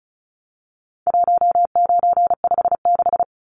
Morse code, heard as a keyed tone, about 34 wpm, 700 Hz. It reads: 1956
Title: Numbers - 4-Digit 35wpm